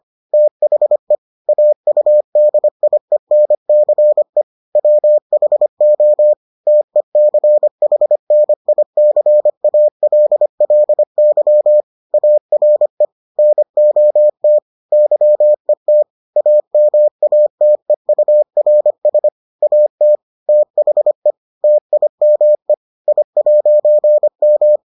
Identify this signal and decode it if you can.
Morse code, heard as a keyed tone, about 25 wpm, 600 Hz. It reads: THE AUDIENCE WHO TECHNICALLY ARE NOT YET AMATEURS AT THE TIME I'M